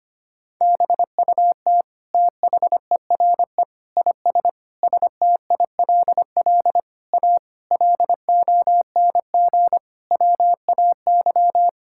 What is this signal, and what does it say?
Morse code, heard as a keyed tone, about 25 wpm, 700 Hz. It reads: BUT THERE IS STILL A LONG WAY
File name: practice / 25wpm / Sentences with 9-Word Phrases - 25wpm